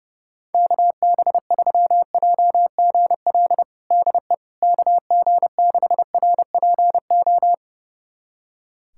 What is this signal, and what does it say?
Morse code, heard as a keyed tone, about 32 wpm, 700 Hz. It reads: KB3JGL DE KG6RPO
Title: USA Callsign DE Callsign 30wpm